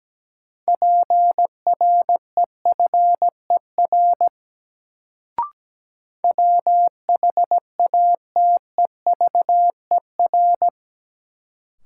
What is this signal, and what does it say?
Morse code, heard as a keyed tone, about 17 wpm, 700 Hz. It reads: PREFER WHATEVER
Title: Sentences from Top 1100 Words - New Word Drill - 17wpm